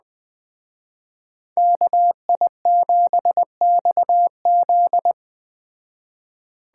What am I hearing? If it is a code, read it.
Morse code, heard as a keyed tone, about 20 wpm, 700 Hz. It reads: KI7XZ
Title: Speed Racing - USA 2x2 Call Signs 17wpm